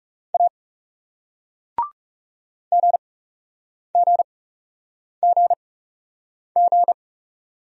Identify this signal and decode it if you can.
Morse code, heard as a keyed tone, about 40 wpm, spaced as for 8 wpm, 700 Hz. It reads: A GGGG